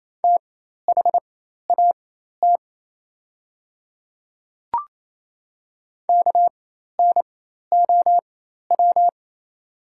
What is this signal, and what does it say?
Morse code, heard as a keyed tone, about 28 wpm, spaced as for 13 wpm, 700 Hz. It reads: THAT KNOW
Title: 4x Character Spacing - Top 500 Words - 28wpm